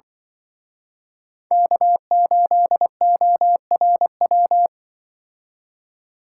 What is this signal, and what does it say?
Morse code, heard as a keyed tone, about 24 wpm, 700 Hz. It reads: K8ORW
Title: Speed Racing - USA 1x3 Call Signs 20wpm